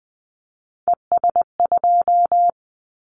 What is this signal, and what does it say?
Morse code, heard as a keyed tone, about 20 wpm, 700 Hz. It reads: ES2
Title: Callsign Prefixes for Top 100 DXCC - 20wpm